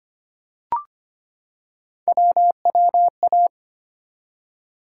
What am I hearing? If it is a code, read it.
Morse code, heard as a keyed tone, about 25 wpm, 700 Hz. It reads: WWA
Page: morse-code-ninja-practice-25wpm